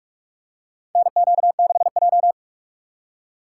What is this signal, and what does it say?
Morse code, heard as a keyed tone, about 46 wpm, 700 Hz. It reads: NQ6J
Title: USA Call Signs - 4 Characters - 3x Character Spacing - 45wpm